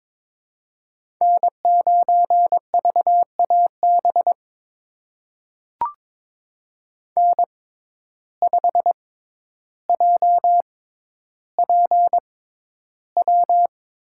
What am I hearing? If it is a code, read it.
Morse code, heard as a keyed tone, about 23 wpm, spaced as for 11 wpm, 700 Hz. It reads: N9VAB N5JPW
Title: USA Call Signs - 5 Characters - 6x Character Spacing - 22wpm